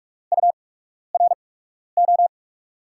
Morse code, heard as a keyed tone, about 46 wpm, spaced as for 12 wpm, 700 Hz. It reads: URO
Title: USA Call Signs - 5 Characters - 8x Character Spacing - 45wpm